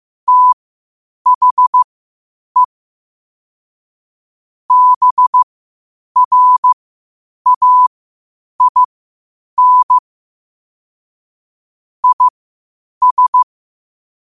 Morse code, heard as a keyed tone, about 15 wpm, spaced as for 8 wpm, 1000 Hz. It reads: THE BRAIN IS
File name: qotdfiles / 8wpm